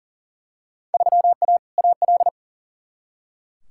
Morse code, heard as a keyed tone, about 40 wpm, 700 Hz. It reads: 3A AL
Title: Field Day Exchanges - 40wpm